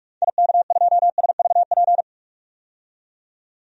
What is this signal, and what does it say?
Morse code, heard as a keyed tone, about 43 wpm, 700 Hz. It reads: IK2SVP
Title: DXCC-100 Call Signs - 6 Characters - 45wpm